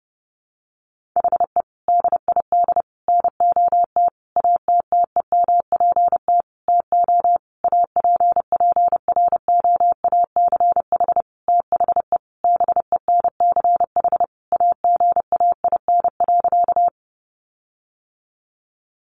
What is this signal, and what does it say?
Morse code, heard as a keyed tone, about 28 wpm, 700 Hz. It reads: HE DID NOT ATTEMPT TO APPROACH THE BENCH AGAIN.
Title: Sentences from Top 1400 Words 30wpm